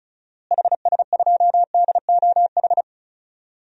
Morse code, heard as a keyed tone, about 36 wpm, 700 Hz. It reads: HS2DOH
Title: DXCC-100 Call Signs - 6 Characters - Repeated 2x - 35wpm